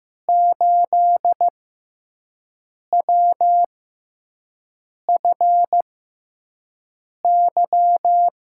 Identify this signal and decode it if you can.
Morse code, heard as a keyed tone, about 15 wpm, spaced as for 5 wpm, 700 Hz. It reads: 8WFY
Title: USA Call Signs - 5 Characters - 6x Character Spacing - 15wpm